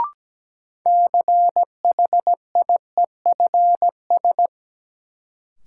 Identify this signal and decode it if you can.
Morse code, heard as a keyed tone, about 17 wpm, 700 Hz. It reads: CHIEFS
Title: Sets of 5 Words Encore - New Word Drill - 17wpm